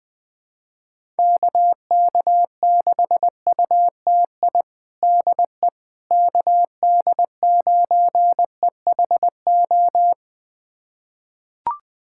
Morse code, heard as a keyed tone, about 21 wpm, 700 Hz. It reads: KK6UTI DE KD9EHO
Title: USA Callsign DE Callsign 20wpm